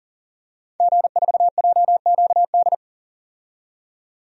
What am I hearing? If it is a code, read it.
Morse code, heard as a keyed tone, about 40 wpm, 700 Hz. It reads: G4JQD
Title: DXCC-100 Call Signs - 5 Characters - 40wpm